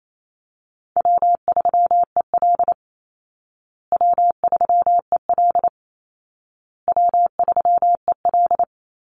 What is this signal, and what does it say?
Morse code, heard as a keyed tone, about 28 wpm, 700 Hz. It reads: W3EL W3EL W3EL
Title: Easy Call Signs Repeated 3x - 28wpm